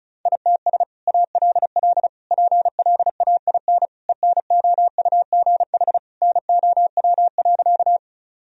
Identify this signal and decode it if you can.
Morse code, heard as a keyed tone, about 36 wpm, 700 Hz. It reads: ITS ALL PLAIN ENOUGH NOW.
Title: Sentences from Top 1000 Words - Review Part 2 - 35wpm